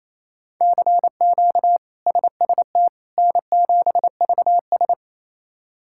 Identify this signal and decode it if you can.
Morse code, heard as a keyed tone, about 28 wpm, 700 Hz. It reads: CQ SST N7VS
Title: K1USN SST QSO Practice - 28wpm